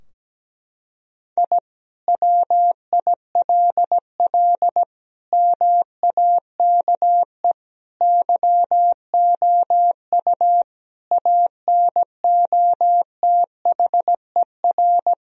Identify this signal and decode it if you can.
Morse code, heard as a keyed tone, about 17 wpm, 700 Hz. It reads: I WILL MAKE YOU ANOTHER
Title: Sentences from Top 1000 Words - Review Part 4 - 17wpm